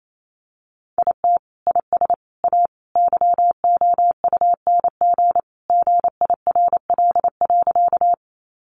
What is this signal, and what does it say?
Morse code, heard as a keyed tone, about 28 wpm, 700 Hz. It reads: IT IS A YOUNG GIRL.
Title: Sentences from Top 400 Words 28wpm